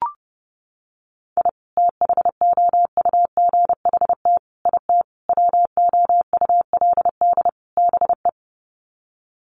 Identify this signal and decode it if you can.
Morse code, heard as a keyed tone, about 30 wpm, 700 Hz. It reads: I THOUGHT IT WOULD BE
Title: Sets of 5 Words Encore 30wpm